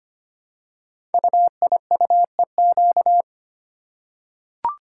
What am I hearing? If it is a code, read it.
Morse code, heard as a keyed tone, about 25 wpm, 700 Hz. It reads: UIUEQ
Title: Code Groups - 5 Letters - 25wpm